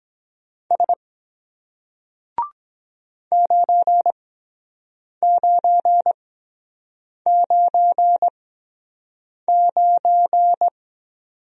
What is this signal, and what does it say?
Morse code, heard as a keyed tone, about 23 wpm, spaced as for 7 wpm, 700 Hz. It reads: S 9999